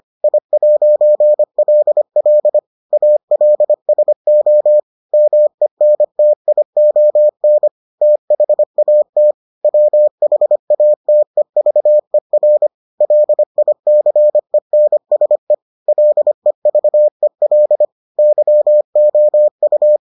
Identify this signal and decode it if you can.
Morse code, heard as a keyed tone, about 25 wpm, 600 Hz. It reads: I'LL ALSO MENTION THAT WHATEVER LICENSE LEVEL YOU